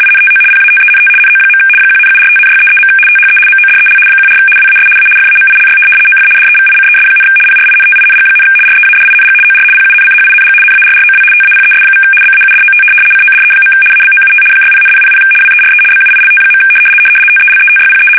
RTTY, about 75 baud. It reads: INTO PATENT NO. 1,745,633. THE MODEL 14 WAS A FAMILY OF DEVICES, PRINTING, READING OR PUNCHING NARROW TAPES= BAUDOT CODE. ABOUT 60,000 WERE BUILT. TELETYPE
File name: fsk850-75B / teletype-corp